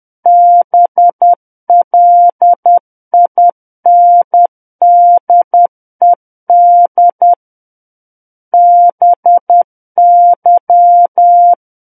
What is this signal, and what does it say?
Morse code, heard as a keyed tone, about 10 wpm, 700 Hz. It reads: BLINDED BY
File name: Book One / 01-Chapter 1 - 10 WPM